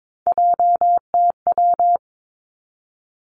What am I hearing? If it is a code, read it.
Morse code, heard as a keyed tone, about 22 wpm, 700 Hz. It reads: JTW
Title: Three Challenging Letter-Number Combinations 22wpm